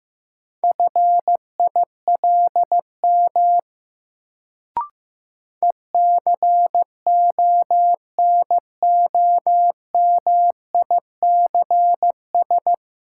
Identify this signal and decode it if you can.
Morse code, heard as a keyed tone, about 15 wpm, 700 Hz. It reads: FILM ECONOMICS
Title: Sentences from Top 1000 Words - Review Part 3 - New Word Drill - 15wpm